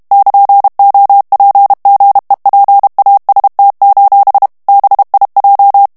Morse code, heard as a keyed tone, about 32 wpm, 800 Hz. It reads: <KN>OPGEPAST: BIJ
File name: HOLLANDAIS texte clair 2 vitesse 30 mots minute